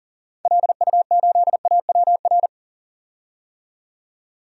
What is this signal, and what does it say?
Morse code, heard as a keyed tone, about 40 wpm, 700 Hz. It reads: LU8AWR
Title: DXCC-100 Call Signs - 6 Characters - 40wpm